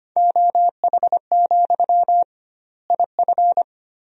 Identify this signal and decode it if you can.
Morse code, heard as a keyed tone, about 25 wpm, 700 Hz. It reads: OH, IF
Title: 2x Word-Spacing - Sentences from Top 500 Words - 25wpm